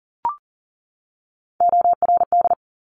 Morse code, heard as a keyed tone, about 42 wpm, 690 Hz. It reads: ORD
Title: Top Three Letter Combinations in Words 40wpm